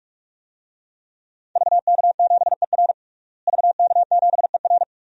Morse code, heard as a keyed tone, about 46 wpm, 700 Hz. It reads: VK7ER VK7ER
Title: DXCC-100 Call Signs - 5 Characters - Repeated 2x - 45wpm